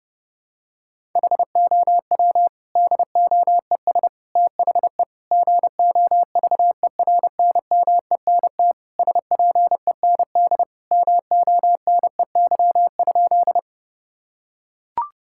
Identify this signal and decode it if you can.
Morse code, heard as a keyed tone, about 30 wpm, 700 Hz. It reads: HOW DOES THE GOVERNMENT SPEND MONEY?
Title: Sentences from Top 1000 Words - Review Part 5 - 30wpm